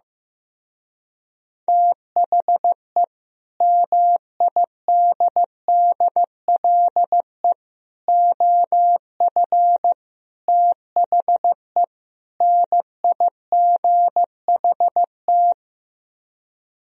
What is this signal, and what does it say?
Morse code, heard as a keyed tone, about 15 wpm, 700 Hz. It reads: THE MIDDLE OF THE NIGHT
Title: Speed-Racing - Sets of 5 Words 15wpm